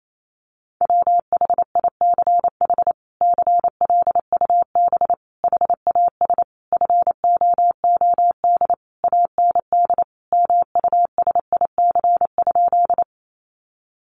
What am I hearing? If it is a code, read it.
Morse code, heard as a keyed tone, about 28 wpm, 700 Hz. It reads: WHICH CLUB HAS FOOD AND MUSIC?
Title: Sentences from Top 1200 Words 28wpm